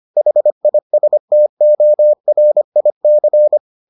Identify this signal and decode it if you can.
Morse code, heard as a keyed tone, about 25 wpm, 600 Hz. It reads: HISTORIC